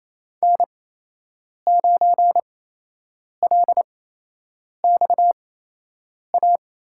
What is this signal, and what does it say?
Morse code, heard as a keyed tone, about 28 wpm, spaced as for 8 wpm, 700 Hz. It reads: N9LXA
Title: USA Call Signs - 5 Characters - 8x Character Spacing - 28wpm